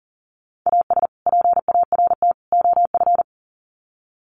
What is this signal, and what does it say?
Morse code, heard as a keyed tone, about 40 wpm, 700 Hz. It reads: AS PART OF
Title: Sets of 3 Words 40wpm